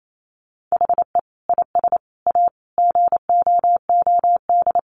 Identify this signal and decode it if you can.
Morse code, heard as a keyed tone, about 28 wpm, 700 Hz. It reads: HE IS A GOOD
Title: Sentences from Top 1100 Words 28wpm